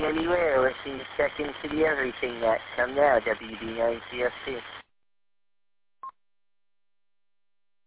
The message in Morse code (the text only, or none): none